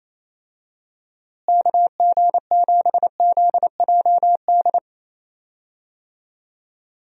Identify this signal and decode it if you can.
Morse code, heard as a keyed tone, about 28 wpm, 700 Hz. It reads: KG7ZJD
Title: USA Call Signs - 6 Characters - Repeated 3x - 28wpm